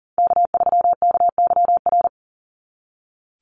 Morse code, heard as a keyed tone, about 40 wpm, 700 Hz. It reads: K3KYR
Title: NA SOTA Chaser Callsigns 40wpm